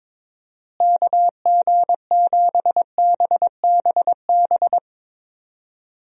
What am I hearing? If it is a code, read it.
Morse code, heard as a keyed tone, about 22 wpm, 700 Hz. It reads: KG7BBB
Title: USA Call Signs - 6 Characters - Repeated 2x - 22wpm